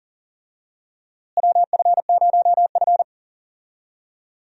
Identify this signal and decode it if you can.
Morse code, heard as a keyed tone, about 38 wpm, 700 Hz. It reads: WF0F
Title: USA Call Signs - 4 Characters - Repeated 2x - 40wpm